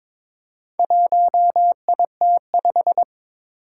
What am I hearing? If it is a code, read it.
Morse code, heard as a keyed tone, about 22 wpm, 700 Hz. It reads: 1IT5
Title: Four Challenging Letter-Number Combinations 22wpm